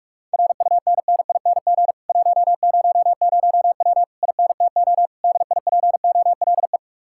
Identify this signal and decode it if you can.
Morse code, heard as a keyed tone, about 56 wpm, 700 Hz. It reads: RUNNING 100W INTO DIPOLE
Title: Speed-Racing Common QSO Phrases - 45wpm